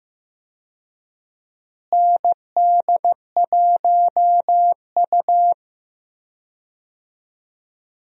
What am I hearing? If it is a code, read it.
Morse code, heard as a keyed tone, about 15 wpm, 700 Hz. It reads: ND1U
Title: USA Call Signs - 4 Characters - Repeated 2x - 15wpm